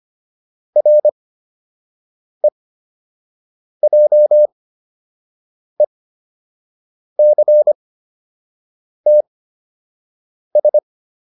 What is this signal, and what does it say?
Morse code, heard as a keyed tone, about 25 wpm, spaced as for 6 wpm, 600 Hz. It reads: REJECTS
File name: RE7 06